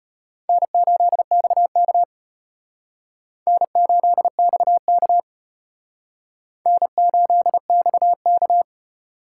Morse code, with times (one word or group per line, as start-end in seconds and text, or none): 0.5-2.0 N8XK
3.5-5.2 N8XK
6.7-8.6 N8XK